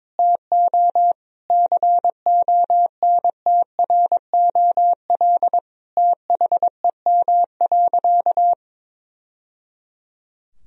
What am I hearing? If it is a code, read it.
Morse code, heard as a keyed tone, about 22 wpm, 700 Hz. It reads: TO CONTROL THEM.
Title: Sentences from Top 1000 Words - Review Part 2 - 22wpm